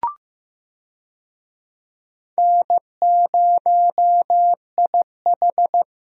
Morse code, heard as a keyed tone, about 15 wpm, 700 Hz. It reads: N0IH
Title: USA Call Signs - 4 Characters - Repeated 2x - 15wpm